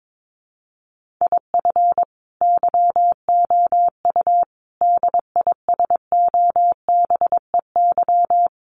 Morse code, heard as a keyed tone, about 22 wpm, 700 Hz. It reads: IF YOU DISOBEY
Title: A Sound Of Thunder 22wpm